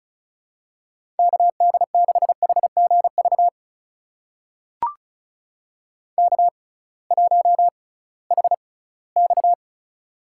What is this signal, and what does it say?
Morse code, heard as a keyed tone, about 36 wpm, spaced as for 20 wpm, 700 Hz. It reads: KD6HGV K1HX
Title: USA Call Signs - 6x Character Spacing - 35wpm